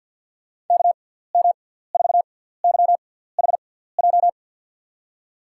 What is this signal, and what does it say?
Morse code, heard as a keyed tone, about 53 wpm, spaced as for 17 wpm, 700 Hz. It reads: KM4YHJ
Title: USA Call Signs - 6 Characters - 6x Character Spacing - 50wpm